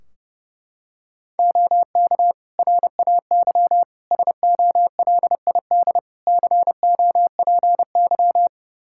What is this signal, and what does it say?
Morse code, heard as a keyed tone, about 30 wpm, 700 Hz. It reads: OK RAY SOLID COPY
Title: Practice QSO - Exchange 3 - More Info - 30wpm